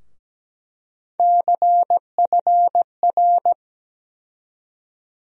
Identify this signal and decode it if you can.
Morse code, heard as a keyed tone, about 17 wpm, 700 Hz. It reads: CFR
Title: USA Call Sign Postfixes - 3 Letters 17wpm